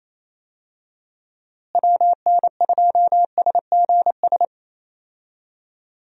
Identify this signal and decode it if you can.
Morse code, heard as a keyed tone, about 28 wpm, 700 Hz. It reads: WN2SGS
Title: USA Call Signs - 6 Characters - Repeated 3x - 28wpm